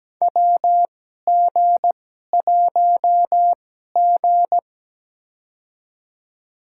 Morse code, heard as a keyed tone, about 17 wpm, spaced as for 12 wpm, 700 Hz. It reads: WG1G
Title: USA Call Signs - 2x Character Spacing - 17wpm